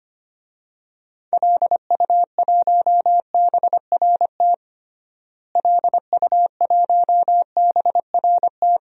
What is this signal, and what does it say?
Morse code, heard as a keyed tone, about 25 wpm, 700 Hz. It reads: LU1BRT LU1BRT
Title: DXCC-100 Call Signs - 6 Characters - Repeated 3x - 25wpm